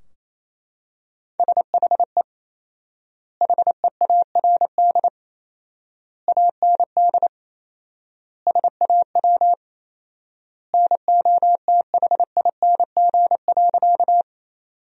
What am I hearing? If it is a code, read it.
Morse code, heard as a keyed tone, about 28 wpm, 700 Hz. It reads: SHE HEARD AND SAW NOTHING.
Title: 4x Word-Spacing - Sentences from Top 500 Words - 28wpm